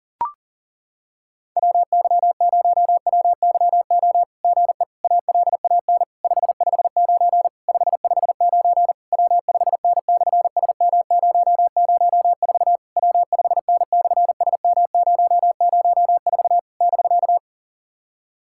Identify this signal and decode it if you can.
Morse code, heard as a keyed tone, about 40 wpm, 700 Hz. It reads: WY0WYO GE ALAN 559 559 W5N/SM004 W5N/SM004 <BK>
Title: SOTA QSO Practice 40wpm